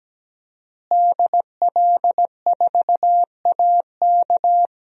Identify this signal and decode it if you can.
Morse code, heard as a keyed tone, about 17 wpm, 700 Hz. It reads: DL4AK
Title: DXCC-100 Call Signs - 5 Characters - Repeated 2x - 17wpm